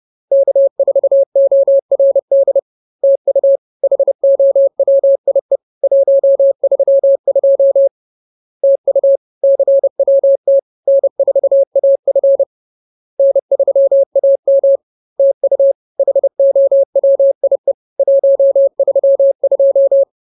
Morse code, heard as a keyed tone, about 30 wpm, 550 Hz. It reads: K4ORD TU HOWIE 132 TU CWT N4AF N3AM TU HOWIE 132
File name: CWT-Chapter-03